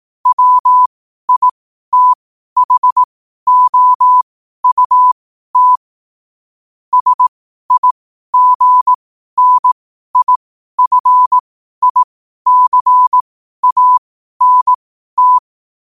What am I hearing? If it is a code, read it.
Morse code, heard as a keyed tone, about 18 wpm, spaced as for 12 wpm, 1000 Hz. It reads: WITHOUT SIGNIFICANT
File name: qotdfiles / 12wpm